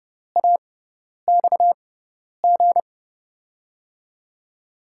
Morse code, heard as a keyed tone, about 30 wpm, spaced as for 10 wpm, 700 Hz. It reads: AXG
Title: USA Call Signs - 5 Characters - 6x Character Spacing - 30wpm